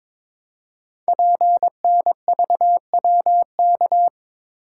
Morse code, heard as a keyed tone, about 22 wpm, 700 Hz. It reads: PNVWK
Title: Code Groups - 5 Letters - 22wpm